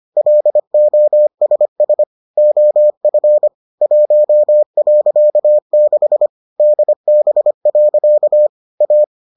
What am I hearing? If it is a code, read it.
Morse code, heard as a keyed tone, about 25 wpm, 600 Hz. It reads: LOSS OF 1.6 DB. A